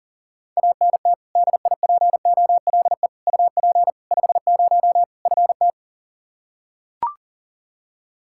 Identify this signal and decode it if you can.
Morse code, heard as a keyed tone, about 37 wpm, 700 Hz. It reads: ANT DIPOLE UP 50 FT
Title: Practice QSO - Exchange 2 - Info - 40wpm